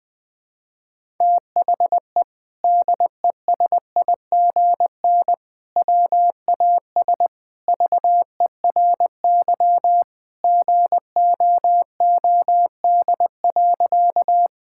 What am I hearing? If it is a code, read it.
Morse code, heard as a keyed tone, about 19 wpm, 700 Hz. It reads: THE DESIGN WAS VERY GOOD.
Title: Sentences from Top 1000 Words - Review Part 5 - 20wpm